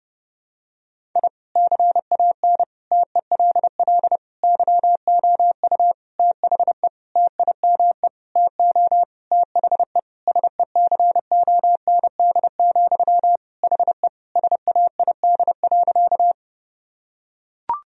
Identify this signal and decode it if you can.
Morse code, heard as a keyed tone, about 32 wpm, 700 Hz. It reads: I CAN TELL YOU THE TIME TO THE SECOND, HE SAID.
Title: Sentences from Top 1000 Words - Review Part 1 - 30wpm